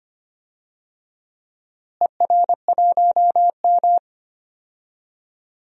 Morse code, heard as a keyed tone, about 25 wpm, 700 Hz. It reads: ER1M
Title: DXCC-100 Call Signs - 4 Characters - Repeated 3x - 25wpm